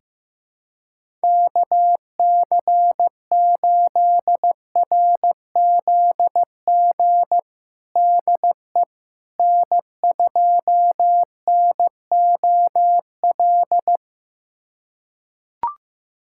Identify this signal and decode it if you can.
Morse code, heard as a keyed tone, about 16 wpm, 700 Hz. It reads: KC8RZG DE N2NOL
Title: USA Callsign DE Callsign 15wpm